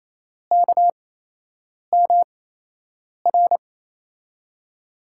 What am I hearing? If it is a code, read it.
Morse code, heard as a keyed tone, about 28 wpm, spaced as for 8 wpm, 700 Hz. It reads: KMR
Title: USA Call Signs - 6 Characters - 8x Character Spacing - 28wpm